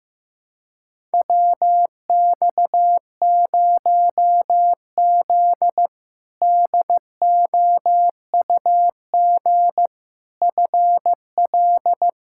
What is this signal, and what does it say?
Morse code, heard as a keyed tone, about 15 wpm, 700 Hz. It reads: WX0Z DOUG FL
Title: K1USN SST QSO Practice - 15wpm